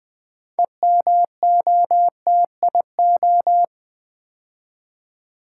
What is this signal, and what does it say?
Morse code, heard as a keyed tone, about 20 wpm, 700 Hz. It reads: EMOTIO
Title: Build A Word - 3 Letters At A Time - 9-Letter Words - 20wpm